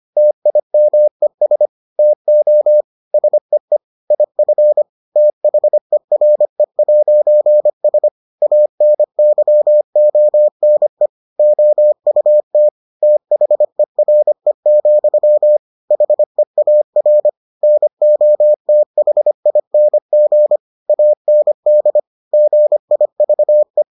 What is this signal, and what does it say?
Morse code, heard as a keyed tone, about 25 wpm, 600 Hz. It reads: TIMES TO SEE IF THERE'S ANYONE OUT THERE, HEAR NOTHING AND GIVE